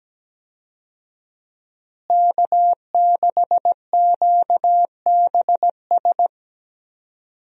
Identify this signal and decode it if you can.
Morse code, heard as a keyed tone, about 17 wpm, 700 Hz. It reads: K6QBS
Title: USA Call Signs - 5 Characters - Repeated 2x - 17wpm